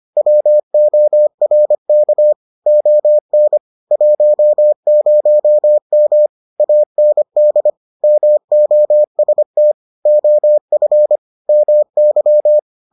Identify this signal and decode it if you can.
Morse code, heard as a keyed tone, about 25 wpm, 600 Hz. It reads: WORK ON 10M AND MOST OF MY